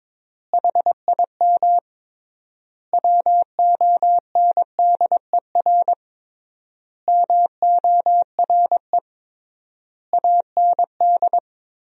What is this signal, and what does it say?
Morse code, heard as a keyed tone, about 23 wpm, 700 Hz. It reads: HIM WONDER MORE AND
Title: 3x Word-Spacing - Sentences from Top 500 Words - 22wpm